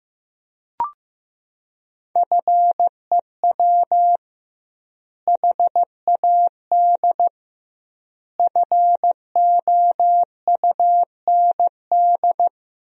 Morse code, heard as a keyed tone, about 15 wpm, 700 Hz. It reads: FEW HAD FOUND
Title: 2x Word-Spacing - Sentences from Top 300 Words - 15wpm